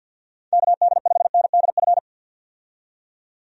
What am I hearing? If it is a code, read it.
Morse code, heard as a keyed tone, about 51 wpm, 700 Hz. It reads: KD5NDF
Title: USA Call Signs - 6 Characters - 50wpm